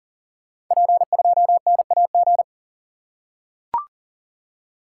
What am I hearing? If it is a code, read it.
Morse code, heard as a keyed tone, about 42 wpm, 700 Hz. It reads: P2NAG
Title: Code Groups - 5 Letter:Numbers - 40wpm